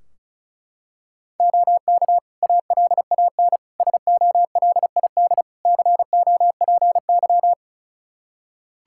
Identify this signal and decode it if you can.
Morse code, heard as a keyed tone, about 36 wpm, 700 Hz. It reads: OK ALAN SOLID COPY
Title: Practice QSO - Exchange 3 - More Info - 35wpm